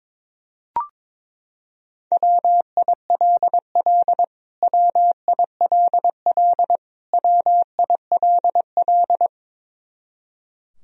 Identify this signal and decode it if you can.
Morse code, heard as a keyed tone, about 22 wpm, 700 Hz. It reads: WILL WILL WILL
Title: Top 100 Words Repeated 22wpm